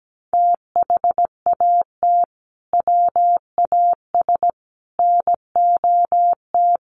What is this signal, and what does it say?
Morse code, heard as a keyed tone, about 17 wpm, 700 Hz. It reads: THAT WAS NOT
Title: Sentences from Top 1400 Words 17wpm